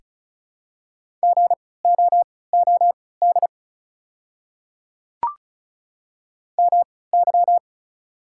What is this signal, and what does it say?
Morse code, heard as a keyed tone, about 36 wpm, spaced as for 20 wpm, 700 Hz. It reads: GOOD MY
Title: 3x Character Spacing - Top 500 Words - 35wpm